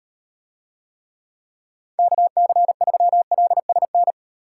Morse code, heard as a keyed tone, about 38 wpm, 700 Hz. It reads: KC3LSN